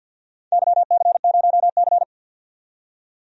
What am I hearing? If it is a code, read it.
Morse code, heard as a keyed tone, about 50 wpm, 700 Hz. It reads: YC0C
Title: DXCC-100 Call Signs - 4 Characters - 50wpm